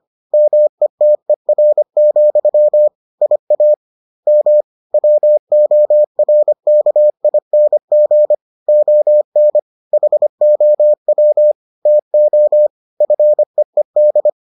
Morse code, heard as a keyed tone, about 25 wpm, 600 Hz. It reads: METER, IAM WORKING ON HOW TO FEED